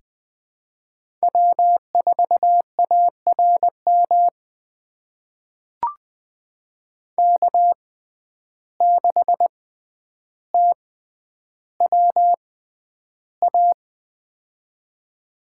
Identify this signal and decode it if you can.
Morse code, heard as a keyed tone, about 20 wpm, spaced as for 10 wpm, 700 Hz. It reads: W4ARM K6TWA
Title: USA Call Signs - 5 Characters - 6x Character Spacing - 20wpm